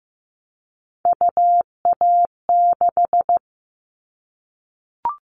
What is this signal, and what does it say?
Morse code, heard as a keyed tone, about 15 wpm, 700 Hz. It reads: UA6
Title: Callsign Prefixes for Top 25 DXCC - Repeated 3x - 15wpm